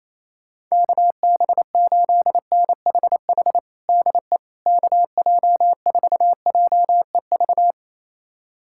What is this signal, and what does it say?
Morse code, heard as a keyed tone, about 28 wpm, 700 Hz. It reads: KB8NHH DE KJ4JEV
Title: USA Callsign DE Callsign 28wpm